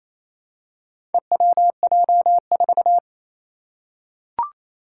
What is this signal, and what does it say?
Morse code, heard as a keyed tone, about 27 wpm, 700 Hz. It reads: EWJ4
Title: Four Challenging Letter-Number Combinations 28wpm